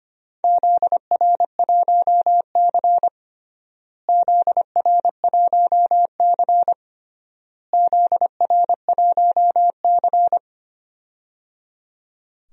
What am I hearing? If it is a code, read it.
Morse code, heard as a keyed tone, about 25 wpm, 700 Hz. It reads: ZR1C ZR1C ZR1C
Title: DXCC-100 Call Signs - 4 Characters - Repeated 3x - 25wpm